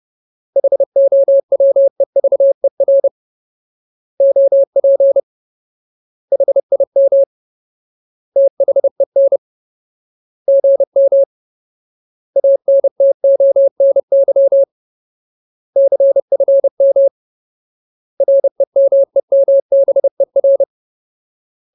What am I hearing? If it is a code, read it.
Morse code, heard as a keyed tone, about 30 wpm, 550 Hz. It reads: HOWEVER OP HIM THEN GM ANTONY CFM REMEMBER
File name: Ultimate Word List / 30WPM Ultimate